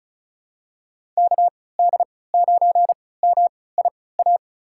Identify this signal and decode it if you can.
Morse code, heard as a keyed tone, about 35 wpm, spaced as for 20 wpm, 700 Hz. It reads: KD9MIA
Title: USA Call Signs - 6 Characters - 3x Character Spacing - 35wpm